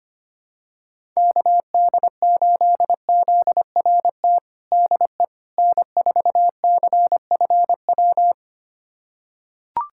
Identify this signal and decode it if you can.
Morse code, heard as a keyed tone, about 25 wpm, 700 Hz. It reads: KD8ZRT DE N4CFW
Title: USA Callsign DE Callsign 25wpm